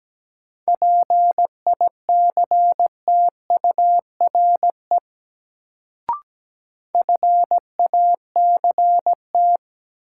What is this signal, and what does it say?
Morse code, heard as a keyed tone, about 17 wpm, 700 Hz. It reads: PICTURE FACT
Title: Sentences from Top 500 Words - New Word Drill - 17wpm